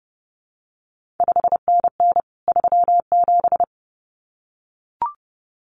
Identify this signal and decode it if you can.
Morse code, heard as a keyed tone, about 30 wpm, 700 Hz. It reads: 5NN 37
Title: CQ WW DX Contest Exchange Encore - 30wpm